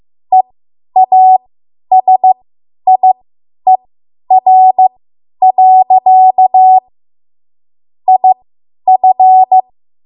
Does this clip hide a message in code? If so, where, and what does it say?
Yes: Morse code from 0.3 to 9.6 s, heard as a keyed tone, about 15 wpm, spaced as for 10 wpm, 750 Hz. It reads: EASIER. IF